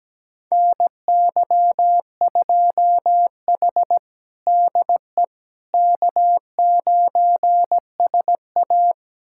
Morse code, heard as a keyed tone, about 17 wpm, 700 Hz. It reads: NY2H DE K9SA
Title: USA Callsign DE Callsign - 4 Characters - 17wpm